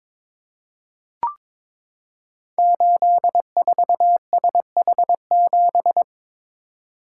Morse code, heard as a keyed tone, about 22 wpm, 700 Hz. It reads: 84SH7
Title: Code Groups - 5 Letter:Numbers - 22wpm